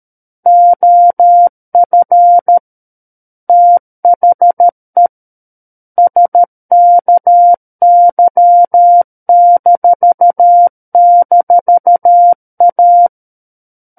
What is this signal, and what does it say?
Morse code, heard as a keyed tone, about 13 wpm, 700 Hz. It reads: OF THE SKY--A
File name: Book One / 04-Chapter 4 - 13 WPM